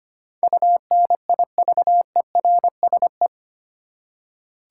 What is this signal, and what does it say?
Morse code, heard as a keyed tone, about 25 wpm, 700 Hz. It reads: UNIVERSE
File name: Eight Letter Words - 25wpm